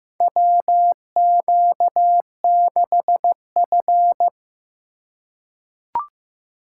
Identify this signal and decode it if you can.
Morse code, heard as a keyed tone, about 15 wpm, 700 Hz. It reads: WQ6F
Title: USA Call Signs - 8x Character Spacing - 15wpm